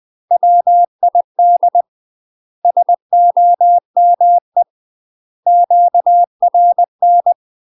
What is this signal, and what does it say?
Morse code, heard as a keyed tone, about 20 wpm, 700 Hz. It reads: WID SOME QRN
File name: UK MASTER QSOs TRAINING / 20WPM / UK MASTER QSOs 0063